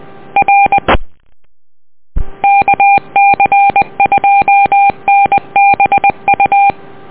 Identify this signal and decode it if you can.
Morse code, heard as a keyed tone, about 20 wpm, 800 Hz. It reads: R KC2NBU
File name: net audio